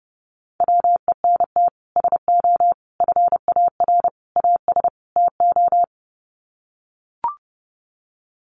Morse code, heard as a keyed tone, about 30 wpm, 700 Hz. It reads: WENT SO FAR AS TO
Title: Sets of 5 Words Encore 30wpm